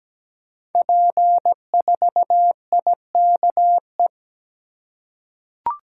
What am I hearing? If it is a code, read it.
Morse code, heard as a keyed tone, about 17 wpm, 700 Hz. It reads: P4IKE
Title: Code Groups - 5 Letter:Numbers - 17wpm